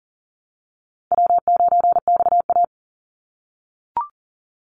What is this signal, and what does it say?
Morse code, heard as a keyed tone, about 41 wpm, 700 Hz. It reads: W9XA
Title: Easy Call Signs - 40wpm